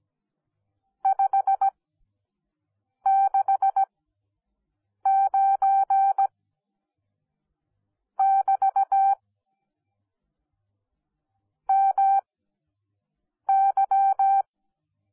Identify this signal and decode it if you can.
Morse code, heard as a keyed tone, about 17 wpm, spaced as for 6 wpm, 780 Hz. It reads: 569 <BT> MY